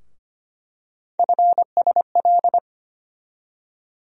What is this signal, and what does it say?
Morse code, heard as a keyed tone, about 25 wpm, 700 Hz. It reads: FSL
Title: USA Call Sign Postfixes - 3 Letters 25wpm